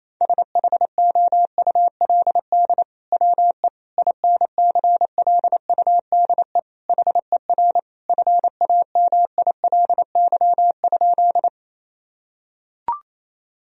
Morse code, heard as a keyed tone, about 28 wpm, 700 Hz. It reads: SHOULD WE INCLUDE HER FAMILY?
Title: Sentences from Top 1000 Words - Review Part 3 - 28wpm